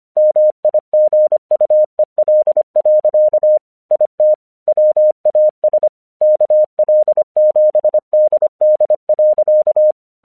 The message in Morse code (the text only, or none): MIGUEL. IT WAS KL7DD.